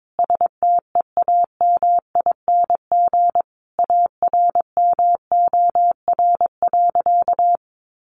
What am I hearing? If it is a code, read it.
Morse code, heard as a keyed tone, about 22 wpm, 700 Hz. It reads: STEAMING ARMOR.
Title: A Sound Of Thunder 22wpm